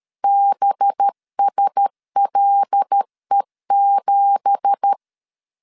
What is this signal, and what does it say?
Morse code, heard as a keyed tone, about 13 wpm, 800 Hz. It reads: BSLE7
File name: CW / DET / 1 9 1500C S C 12wpm 3811